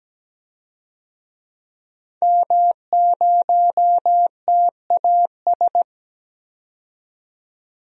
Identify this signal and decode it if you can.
Morse code, heard as a keyed tone, about 17 wpm, 700 Hz. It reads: M0TAS
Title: DXCC-100 Call Signs - 5 Characters - 17wpm